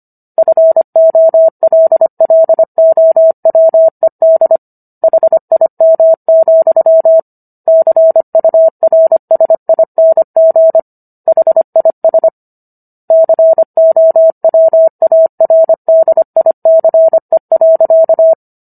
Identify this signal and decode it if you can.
Morse code, heard as a keyed tone, about 25 wpm, 650 Hz. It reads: FOLLOWED HIM, CURSING HIS COWARDICE.